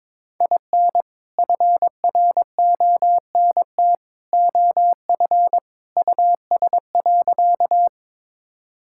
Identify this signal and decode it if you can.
Morse code, heard as a keyed tone, about 22 wpm, 700 Hz. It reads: IN FRONT OF US.
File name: practice / 22wpm / Sentences with 2-Word Phrases - 22wpm